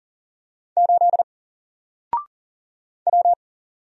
Morse code, heard as a keyed tone, about 40 wpm, 700 Hz. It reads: 8 W
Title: Single Character (Letter:Number:Punctuation:Prosign) Alt 40wpm